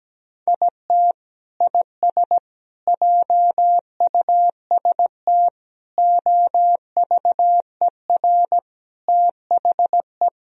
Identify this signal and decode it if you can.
Morse code, heard as a keyed tone, about 17 wpm, 700 Hz. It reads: IT IS JUST OVER THE
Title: Sentences from Top 1000 Words - Review Part 1 - 17wpm